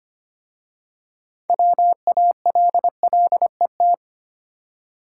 Morse code, heard as a keyed tone, about 25 wpm, 700 Hz. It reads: WALLET
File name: practice / 25wpm / Easy Words with Hints - 25wpm